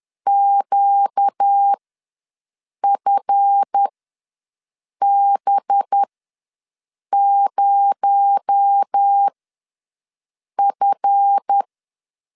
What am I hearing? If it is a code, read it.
Morse code, heard as a keyed tone, about 10 wpm, spaced as for 5 wpm, 800 Hz. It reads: QFB0F
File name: LEZIONI / 7 GF9Z / 7 GF ws 10wpm 3918